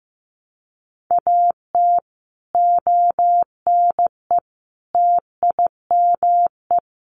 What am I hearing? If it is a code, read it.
Morse code, heard as a keyed tone, about 15 wpm, 700 Hz. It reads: AT ONE TIME
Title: Sets of 5 Words Encore 15wpm